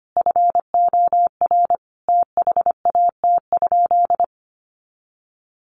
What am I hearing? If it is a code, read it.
Morse code, heard as a keyed tone, about 25 wpm, 700 Hz. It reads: FOR THAT?
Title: Sentences from Top 300 Words 25wpm